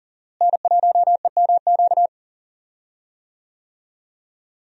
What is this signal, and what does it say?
Morse code, heard as a keyed tone, about 40 wpm, 700 Hz. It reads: N1EMQ
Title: USA Call Signs - 5 Characters - Repeated 2x - 40wpm